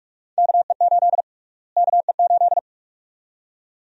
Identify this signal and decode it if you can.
Morse code, heard as a keyed tone, about 44 wpm, 700 Hz. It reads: KE8 KE8
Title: USA Call Sign Prefixes - Repeated 2x - 45wpm